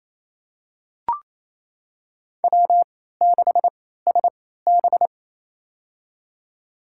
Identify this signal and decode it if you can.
Morse code, heard as a keyed tone, about 28 wpm, spaced as for 16 wpm, 700 Hz. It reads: W6SB
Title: USA Call Signs - 4 Characters - 3x Character Spacing - 28wpm